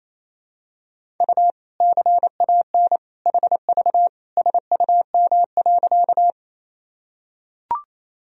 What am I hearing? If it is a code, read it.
Morse code, heard as a keyed tone, about 28 wpm, 700 Hz. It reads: U CAN HV SUM.
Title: Sentences from Top 100 Words and Abbreviations 28wpm